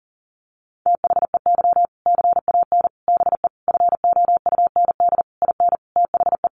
Morse code, heard as a keyed tone, about 40 wpm, 700 Hz. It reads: THEY CAN BE FOUND IN THE
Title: Sentences from Top 1500 Words 40wpm